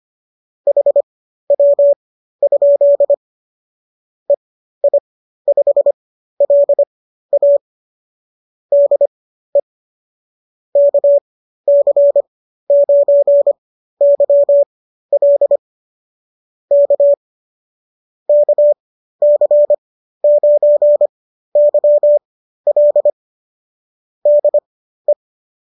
Morse code, heard as a keyed tone, about 25 wpm, spaced as for 13 wpm, 590 Hz. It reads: HW? EI5LA DE KC9YL K KC9YL DE